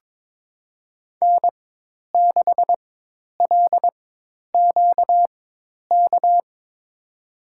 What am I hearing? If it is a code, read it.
Morse code, heard as a keyed tone, about 22 wpm, spaced as for 10 wpm, 700 Hz. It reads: N6LQK
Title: USA Call Signs - 5 Characters - 4x Character Spacing - 22wpm